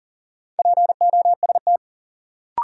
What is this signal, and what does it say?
Morse code, heard as a keyed tone, about 40 wpm, 700 Hz. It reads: POST
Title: Sentences from Top 1000 Words - New Word Drill - 40wpm